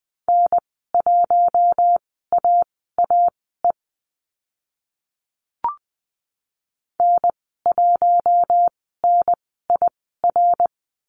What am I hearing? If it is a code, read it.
Morse code, heard as a keyed tone, about 20 wpm, spaced as for 14 wpm, 700 Hz. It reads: N1AAE N1NIR
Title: 082 Lesson 14g - Number 2 - Call Signs 20wpm